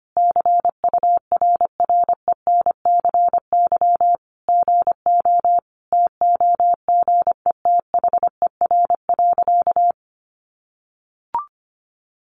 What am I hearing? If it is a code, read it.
Morse code, heard as a keyed tone, about 25 wpm, 700 Hz. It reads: CURRENCY GO TOGETHER.
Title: Sentences from Top 1500 Words 25wpm